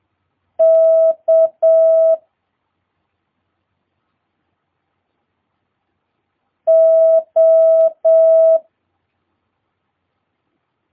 Morse code, heard as a keyed tone, about 7 wpm, 650 Hz. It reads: K O
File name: jh8cbh / 2022cw